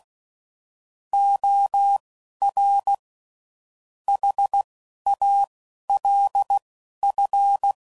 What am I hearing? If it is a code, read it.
Morse code, heard as a keyed tone, about 16 wpm, spaced as for 11 wpm, 800 Hz. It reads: OR HALF